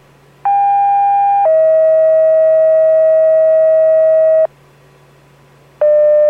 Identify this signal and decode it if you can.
Two-tone page, heard as a one-second tone, then a three-second tone, about 800 then 640 Hz. It is heard from 0.5 to 4.5 s.